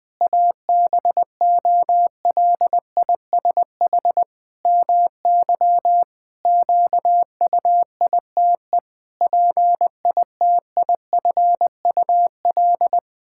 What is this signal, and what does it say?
Morse code, heard as a keyed tone, about 20 wpm, 700 Hz. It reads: ABOLISH MY QUITE PITIFUL
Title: Pangram Challenge v2 - 20wpm